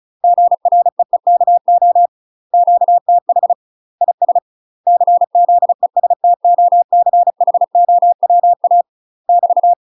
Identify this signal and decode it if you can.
Morse code, heard as a keyed tone, about 34 wpm, 700 Hz. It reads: GREEKO QTH IS CZESTOCHOWA <BT>
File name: UK MASTER QSOs TRAINING / 35WPM / UK MASTER QSOs 0005